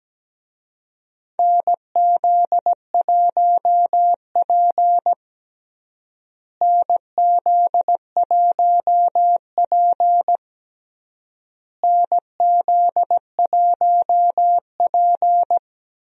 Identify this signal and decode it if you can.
Morse code, heard as a keyed tone, about 17 wpm, 700 Hz. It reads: NZ1P NZ1P NZ1P
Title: USA Call Signs - 4 Characters - Repeated 3x - 17wpm